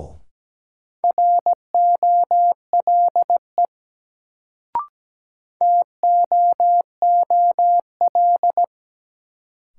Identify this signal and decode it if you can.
Morse code, heard as a keyed tone, about 17 wpm, 700 Hz. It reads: ROLE TOOL
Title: Sentences from Top 1000 Words - Review Part 6 - New Word Drill - 17wpm